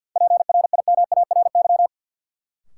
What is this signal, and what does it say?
Morse code, heard as a keyed tone, about 50 wpm, 700 Hz. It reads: PRIMARY
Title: Sentences from Top 1000 Words - Review Part 5 - New Word Drill - 50wpm